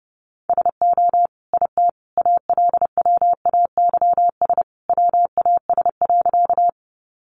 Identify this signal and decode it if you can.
Morse code, heard as a keyed tone, about 30 wpm, 700 Hz. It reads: SO IT ALWAYS WAS.
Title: Sentences from Top 300 Words 30wpm